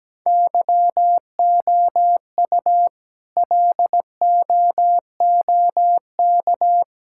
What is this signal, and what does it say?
Morse code, heard as a keyed tone, about 17 wpm, 700 Hz. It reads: YOU LOOK
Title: Sentences from Top 1000 Words - Review Part 5 - 17wpm